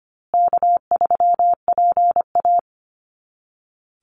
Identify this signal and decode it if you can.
Morse code, heard as a keyed tone, about 25 wpm, 700 Hz. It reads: K3PA
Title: Call Signs Repeated 2x - 25wpm